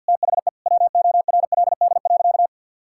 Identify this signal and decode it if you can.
Morse code, heard as a keyed tone, about 50 wpm, 700 Hz. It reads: THE WORLD.
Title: Sentences from Top 1600 Words 50wpm